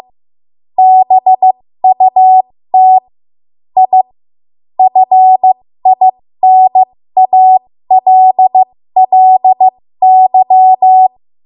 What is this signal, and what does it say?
Morse code, heard as a keyed tone, about 15 wpm, spaced as for 13 wpm, 750 Hz. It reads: BUT I FINALLY
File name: Archive / 13 WPM / 13WPM